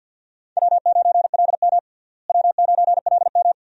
Morse code, heard as a keyed tone, about 46 wpm, 700 Hz. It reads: W9LM W9LM
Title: USA Call Signs - 4 Characters - Repeated 2x - 50wpm